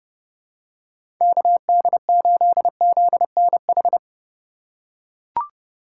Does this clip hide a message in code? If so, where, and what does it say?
Yes: Morse code from 1.2 to 4.0 s, heard as a keyed tone, about 29 wpm, 700 Hz. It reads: KD8ZNH